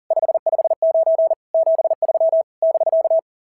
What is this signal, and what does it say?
Morse code, heard as a keyed tone, about 40 wpm, 690 Hz. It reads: 559 73 <BK>
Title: SOTA QSO Practice 40wpm